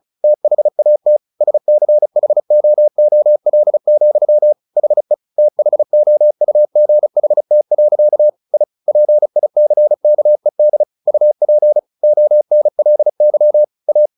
Morse code, heard as a keyed tone, about 34 wpm, 600 Hz. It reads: THAT SCHOOL, HE THOUGHT. I PICKED UP ONLY A
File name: the old man and the sea / the old man and the sea 1 35wpm